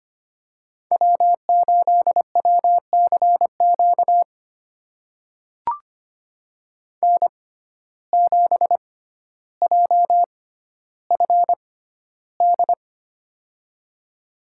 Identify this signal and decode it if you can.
Morse code, heard as a keyed tone, about 25 wpm, spaced as for 13 wpm, 700 Hz. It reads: W8WCQ N7JFD
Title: USA Call Signs - 5 Characters - 6x Character Spacing - 25wpm